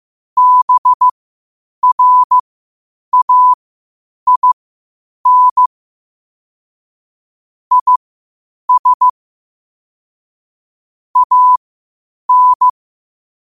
Morse code, heard as a keyed tone, about 15 wpm, spaced as for 8 wpm, 1000 Hz. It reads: BRAIN IS AN
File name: 8wpm